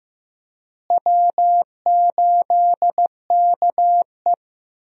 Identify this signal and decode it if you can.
Morse code, heard as a keyed tone, about 15 wpm, 700 Hz. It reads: W8KE
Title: USA Call Signs - 4 Characters - Repeated 3x - 15wpm